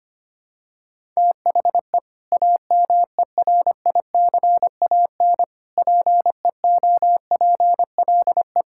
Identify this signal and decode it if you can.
Morse code, heard as a keyed tone, about 25 wpm, 700 Hz. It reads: THE AMERICAN PEOPLE
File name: practice / 25wpm / Sentences with 3-Word Phrases - 25wpm